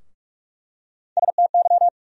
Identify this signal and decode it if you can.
Morse code, heard as a keyed tone, about 46 wpm, 700 Hz. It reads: STY
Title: USA Call Sign Postfixes - 3 Letters 45wpm